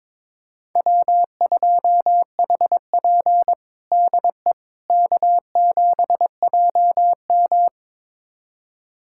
Morse code, heard as a keyed tone, about 22 wpm, 700 Hz. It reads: W2HP DE K7JM
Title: USA Callsign DE Callsign - 4 Characters - 22wpm